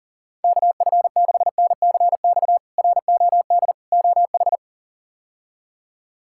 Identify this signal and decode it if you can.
Morse code, heard as a keyed tone, about 40 wpm, 700 Hz. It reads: KF6NCX ROD OH
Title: K1USN SST QSO Practice - 40wpm